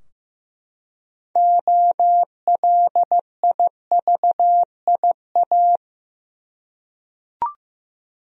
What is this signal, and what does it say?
Morse code, heard as a keyed tone, about 15 wpm, 700 Hz. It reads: OLIVIA
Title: USA Common Names - 15wpm